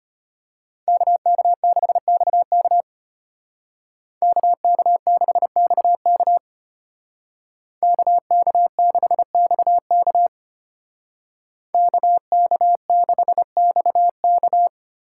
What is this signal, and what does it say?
Morse code, heard as a keyed tone, about 33 wpm, 700 Hz. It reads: KK6XK KK6XK KK6XK KK6XK